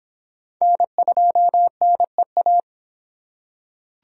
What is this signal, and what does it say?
Morse code, heard as a keyed tone, about 26 wpm, 700 Hz. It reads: N2NEA